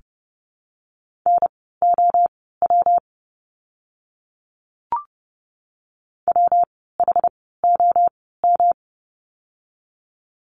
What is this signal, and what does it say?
Morse code, heard as a keyed tone, about 30 wpm, spaced as for 17 wpm, 700 Hz. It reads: NOW WHOM
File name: course / 30:10wpm / 115 Lesson 19f - Letter w - Words 30wpm